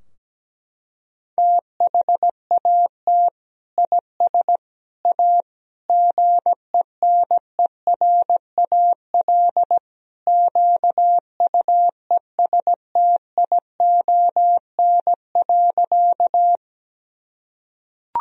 Morse code, heard as a keyed tone, about 17 wpm, 700 Hz. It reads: THAT IS A GENERAL QUESTION.
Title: Sentences from Top 1000 Words - Review Part 3 - 17wpm